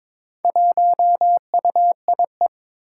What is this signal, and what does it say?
Morse code, heard as a keyed tone, about 21 wpm, 700 Hz. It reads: 1UIE
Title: Four Challenging Letter-Number Combinations 22wpm